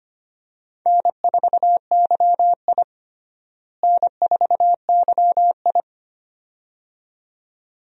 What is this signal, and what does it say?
Morse code, heard as a keyed tone, about 25 wpm, 700 Hz. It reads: N4YI N4YI
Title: USA Call Signs - 4 Characters - Repeated 2x - 25wpm